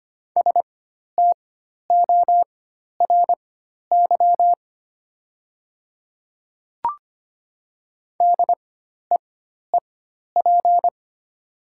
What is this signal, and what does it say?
Morse code, heard as a keyed tone, about 25 wpm, spaced as for 12 wpm, 700 Hz. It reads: STORY DEEP
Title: 4x Character Spacing - Top 500 Words - 25wpm